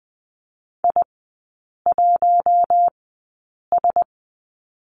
Morse code, heard as a keyed tone, about 20 wpm, spaced as for 8 wpm, 700 Hz. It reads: I1S
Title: 107 Lesson 18e - Letter m - Three Characters 20wpm